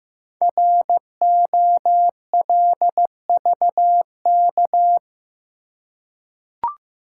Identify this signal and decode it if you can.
Morse code, heard as a keyed tone, about 15 wpm, 700 Hz. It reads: ROLVK
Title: Code Groups - 5 Letter:Numbers - 15wpm